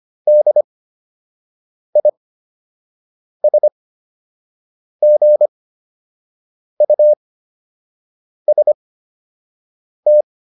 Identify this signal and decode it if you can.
Morse code, heard as a keyed tone, about 25 wpm, spaced as for 6 wpm, 600 Hz. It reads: DISGUST